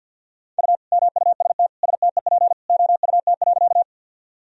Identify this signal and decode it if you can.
Morse code, heard as a keyed tone, about 51 wpm, 700 Hz. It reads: U MUST STEP OUT.
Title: Sentences from Top 400 Words and Abbreviations 50wpm